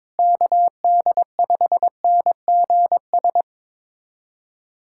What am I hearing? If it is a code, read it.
Morse code, heard as a keyed tone, about 22 wpm, 700 Hz. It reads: KD5NGS
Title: USA Callsign DE Callsign 22wpm